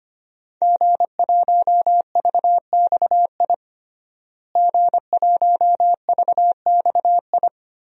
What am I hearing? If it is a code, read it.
Morse code, heard as a keyed tone, about 25 wpm, 700 Hz. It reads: G1VXI G1VXI
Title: DXCC-100 Call Signs - 5 Characters - Repeated 3x - 25wpm